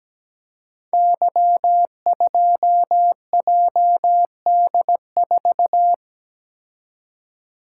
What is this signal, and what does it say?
Morse code, heard as a keyed tone, about 17 wpm, 700 Hz. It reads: Y2JD4
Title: Code Groups - 5 Letter:Numbers - 17wpm